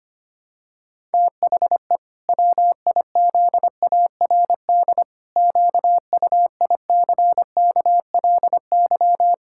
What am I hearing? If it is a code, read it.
Morse code, heard as a keyed tone, about 25 wpm, 700 Hz. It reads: THE WIZARD QUICKLY
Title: Pangram Challenge v2 - 25wpm